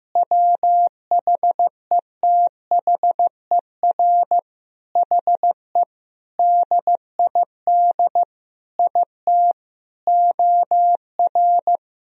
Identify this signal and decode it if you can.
Morse code, heard as a keyed tone, about 15 wpm, 700 Hz. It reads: WHETHER HE DID IT OR
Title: Sentences from Top 1000 Words - Review Part 3 - 15wpm